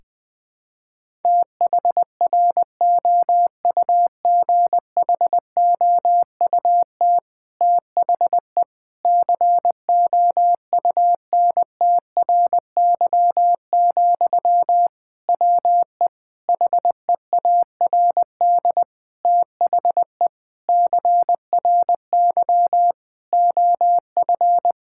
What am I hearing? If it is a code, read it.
Morse code, heard as a keyed tone, about 21 wpm, 700 Hz. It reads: THROUGHOUT THE COUNTRY, WE HEARD THE CRY OF
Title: Sentences from Top 1000 Words - Review Part 5 - 20wpm